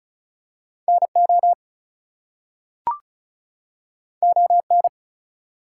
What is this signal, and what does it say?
Morse code, heard as a keyed tone, about 36 wpm, 700 Hz. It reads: NO ON
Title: Two Letter Words - 35wpm